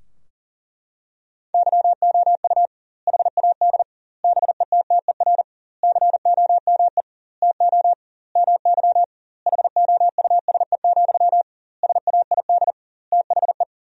Morse code, heard as a keyed tone, about 40 wpm, 700 Hz. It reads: YOU HAD BETTER COME TO MY HOUSE, SAID THE
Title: 2x Word-Spacing - Sentences from Top 400 Words - 40wpm